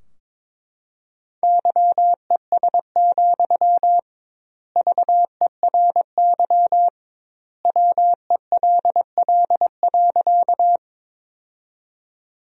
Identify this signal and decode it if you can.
Morse code, heard as a keyed tone, about 22 wpm, 700 Hz. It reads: YES, VERY WELL.
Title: 2x Word-Spacing - Sentences from Top 500 Words - 22wpm